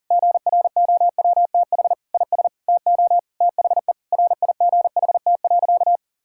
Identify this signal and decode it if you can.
Morse code, heard as a keyed tone, about 40 wpm, 700 Hz. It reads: GROWTH IS TO THE RIGHT.
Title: Sentences from Top 1000 Words - Review Part 5 - 40wpm